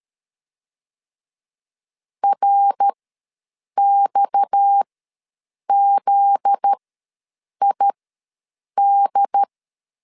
Morse code, heard as a keyed tone, about 13 wpm, spaced as for 7 wpm, 800 Hz. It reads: RXZID
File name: LEZIONI / 7 GF9Z / 7 GF9Z ws 12wpm 3321